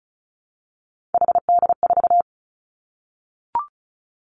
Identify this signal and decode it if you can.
Morse code, heard as a keyed tone, about 36 wpm, 700 Hz. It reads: HD4
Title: Three Challenging Letter-Number Combinations 35wpm